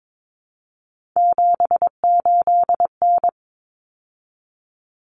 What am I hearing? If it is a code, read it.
Morse code, heard as a keyed tone, about 22 wpm, 700 Hz. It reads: 78N
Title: Common Cut Numbers - 3-Digit 22wpm